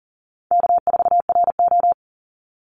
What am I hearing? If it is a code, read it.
Morse code, heard as a keyed tone, about 40 wpm, 700 Hz. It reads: K4RO
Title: Call Signs - 40wpm